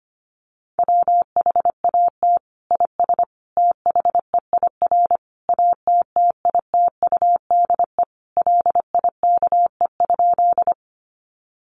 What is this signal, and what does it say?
Morse code, heard as a keyed tone, about 25 wpm, 700 Hz. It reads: WHAT IS THEIR ATTITUDE LIKE?
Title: Sentences from Top 1300 Words 25wpm